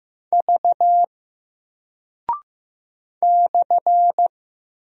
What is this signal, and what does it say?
Morse code, heard as a keyed tone, about 15 wpm, 700 Hz. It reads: V /
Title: Single Character (Letter:Number:Punctuation:Prosign) Alt 15wpm